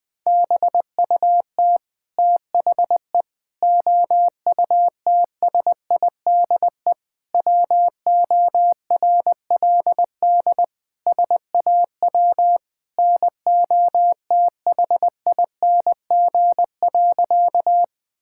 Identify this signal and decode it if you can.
Morse code, heard as a keyed tone, about 20 wpm, 700 Hz. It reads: BUT THE OUTSIDE WORLD SAW NOTHING.
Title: Sentences from Top 1000 Words - Review Part 6 - 20wpm